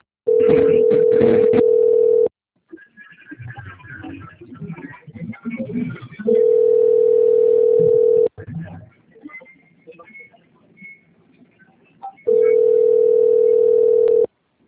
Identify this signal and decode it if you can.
Ringback tone (North America), 440 + 480 Hz, with speech.
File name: out- - - - - .18892